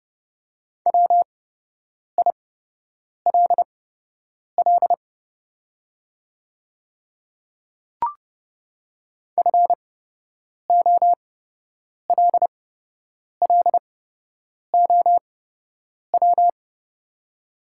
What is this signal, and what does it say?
Morse code, heard as a keyed tone, about 29 wpm, spaced as for 8 wpm, 700 Hz. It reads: WILL FOLLOW
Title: 8x Character Spacing - Top 200 Words - 30wpm